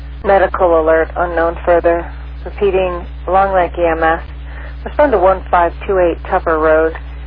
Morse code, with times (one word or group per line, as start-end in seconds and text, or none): none